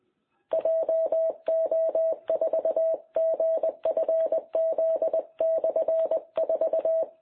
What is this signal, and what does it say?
Morse code, heard as a keyed tone, about 20 wpm, 660 Hz. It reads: JO4GFZ/4
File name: jh8cbh / 2021cw